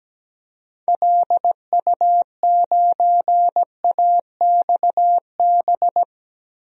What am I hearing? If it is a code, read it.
Morse code, heard as a keyed tone, about 17 wpm, 700 Hz. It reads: LU9AXB
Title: DXCC-100 Call Signs - 6 Characters - Repeated 2x - 17wpm